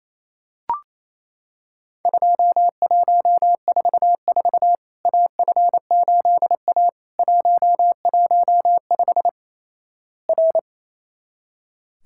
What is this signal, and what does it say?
Morse code, heard as a keyed tone, about 28 wpm, 700 Hz. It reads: 2144 AF8A 115 R
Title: Worked All Europe DX Contest - QTC Exchange - 28wpm